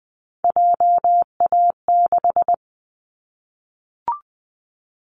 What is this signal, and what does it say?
Morse code, heard as a keyed tone, about 20 wpm, 700 Hz. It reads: JA6
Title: Callsign Prefixes for Top 50 DXCC - 20wpm